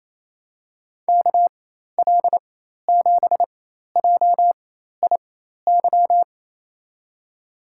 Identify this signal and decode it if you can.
Morse code, heard as a keyed tone, about 28 wpm, spaced as for 13 wpm, 700 Hz. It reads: KL7JIY
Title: USA Call Signs - 6 Characters - 4x Character Spacing - 28wpm